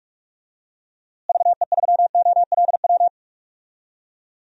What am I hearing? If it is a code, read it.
Morse code, heard as a keyed tone, about 46 wpm, 700 Hz. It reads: VE3OLW
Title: DXCC-100 Call Signs - 6 Characters - Repeated 2x - 45wpm